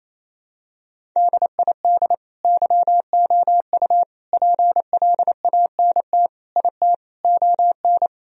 Morse code, heard as a keyed tone, about 28 wpm, 700 Hz. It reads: DID YOU PLANT IT ON
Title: Sentences from Top 1000 Words - Review Part 1 - 28wpm